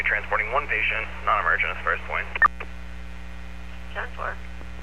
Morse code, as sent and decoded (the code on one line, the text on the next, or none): none